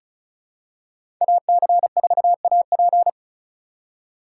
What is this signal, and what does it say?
Morse code, heard as a keyed tone, about 34 wpm, 700 Hz. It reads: AC4AP
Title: USA Call Signs - 5 Characters - 2x Character Spacing - 35wpm